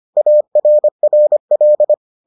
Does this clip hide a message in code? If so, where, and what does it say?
Yes: Morse code from 0.2 to 1.9 s, heard as a keyed tone, about 25 wpm, 600 Hz. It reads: ARRL